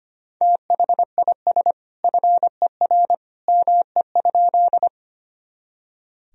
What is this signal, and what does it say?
Morse code, heard as a keyed tone, about 25 wpm, 700 Hz. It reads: THIS FER ME?
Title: Sentences from Top 200 Words and Abbreviations 25wpm